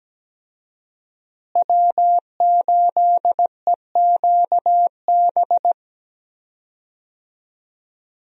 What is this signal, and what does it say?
Morse code, heard as a keyed tone, about 17 wpm, 700 Hz. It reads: W8EQB
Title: USA Call Signs - 5 Characters - Repeated 2x - 17wpm